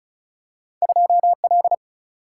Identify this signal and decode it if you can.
Morse code, heard as a keyed tone, about 36 wpm, 700 Hz. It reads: 2L
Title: Single-Letter Single-Number 35wpm